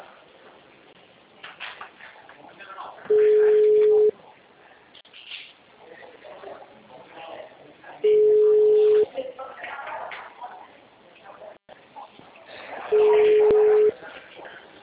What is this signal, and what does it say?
Ringback tone (Europe, 425 Hz), with speech.